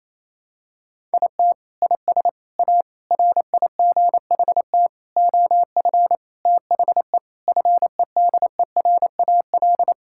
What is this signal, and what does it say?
Morse code, heard as a keyed tone, about 28 wpm, 700 Hz. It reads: IT IS A RIGHT OF THE FEDERAL
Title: Sentences from Top 1000 Words - Review Part 5 - 28wpm